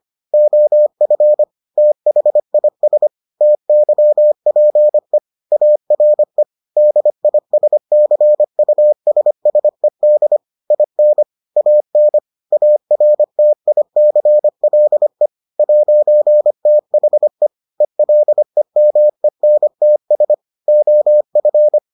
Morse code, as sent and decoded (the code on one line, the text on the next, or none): --- ..-. / - .... .. ... / - -.-- .--. / .- .-. / -.. .. ... -.-. ..- ... ... -.. / .. -. / .- -. / .- .-. - .. -.-. .-.. / .----. - .... / . .-.. -- . -. - ... / --- ..-.
OF THIS TYPE ARE DISCUSSED IN AN ARTICLE 'THE ELEMENTS OF